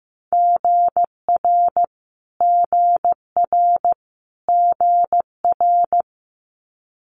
Morse code, heard as a keyed tone, about 15 wpm, 700 Hz. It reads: GR GR GR
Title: Multi-letter Phonemes Repeated 3x 15wpm